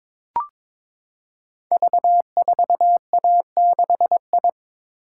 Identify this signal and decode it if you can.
Morse code, heard as a keyed tone, about 22 wpm, 700 Hz. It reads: V4A6I
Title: Code Groups - 5 Letter:Numbers - 22wpm